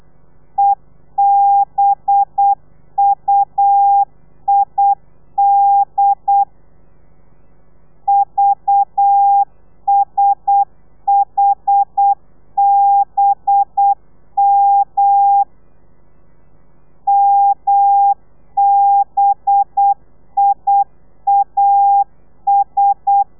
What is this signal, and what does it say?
Morse code, heard as a keyed tone, about 8 wpm, 800 Hz. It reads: EBUID VSHBM MBIAS
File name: telegrafia / i2tfj / Lez4 35